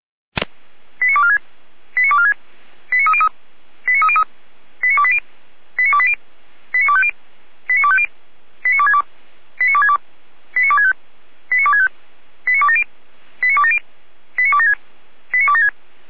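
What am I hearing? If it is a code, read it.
Five-tone selective call (ZVEI), heard as quick runs of five tones, tones of about 70 ms. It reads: 89246 89246 89303 89303 89280 89280 89250 89250 89262 89262 89265 89265 89270 89270 89276 89276
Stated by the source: Five-tone ZVEI selective call received on an SDR, from feuerwehr-orlamuende.org (fire brigade Orlamünde).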